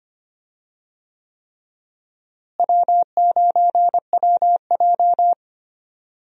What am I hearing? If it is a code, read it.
Morse code, heard as a keyed tone, about 25 wpm, 700 Hz. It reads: W9WJ
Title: USA Call Signs - 4 Characters - Repeated 3x - 25wpm